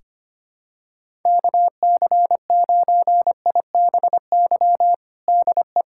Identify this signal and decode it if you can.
Morse code, heard as a keyed tone, about 25 wpm, 700 Hz. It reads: KC9IBY DE
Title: USA Callsign DE Callsign - 6 Characters - 25wpm